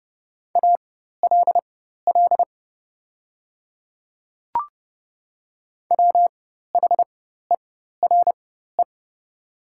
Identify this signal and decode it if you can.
Morse code, heard as a keyed tone, about 30 wpm, spaced as for 14 wpm, 700 Hz. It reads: ALL WHERE
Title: 4x Character Spacing - Top 300 Words - 30wpm